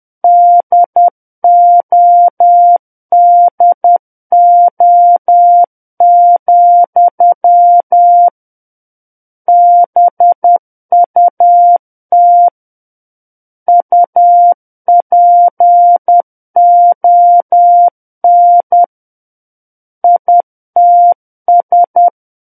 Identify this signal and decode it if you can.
Morse code, heard as a keyed tone, about 10 wpm, 700 Hz. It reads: DODO, BUT UPON ITS
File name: Book One / 01-Chapter 1 - 10 WPM